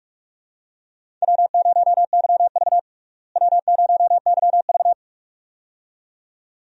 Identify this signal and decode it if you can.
Morse code, heard as a keyed tone, about 42 wpm, 700 Hz. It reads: W0YV W0YV
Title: USA Call Signs - 4 Characters - Repeated 2x - 45wpm